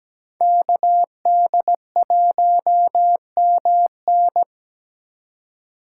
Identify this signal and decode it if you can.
Morse code, heard as a keyed tone, about 20 wpm, 700 Hz. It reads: KD1MN
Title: Speed Racing - USA Call Signs 17wpm